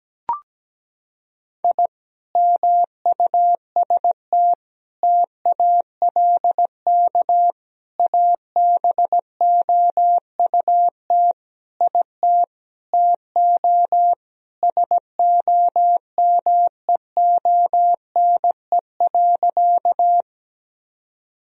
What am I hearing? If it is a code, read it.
Morse code, heard as a keyed tone, about 17 wpm, 700 Hz. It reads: I MUST TALK ABOUT IT TO SOMEONE.
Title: Sentences from Top 1000 Words - Review Part 4 - 17wpm